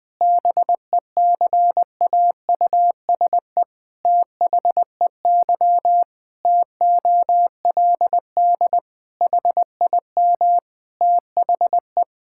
Morse code, heard as a keyed tone, about 20 wpm, 700 Hz. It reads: BECAUSE THEY TOLD HIM THE
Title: Pangram Challenge v2 - 20wpm